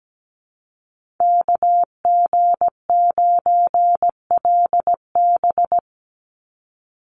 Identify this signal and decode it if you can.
Morse code, heard as a keyed tone, about 17 wpm, 700 Hz. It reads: KG9LB
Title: Call Signs Repeated 3x - 17wpm